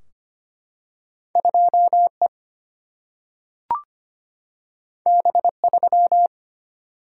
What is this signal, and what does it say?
Morse code, heard as a keyed tone, about 25 wpm, 700 Hz. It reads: 2E B3
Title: Single-Letter Single-Number 25wpm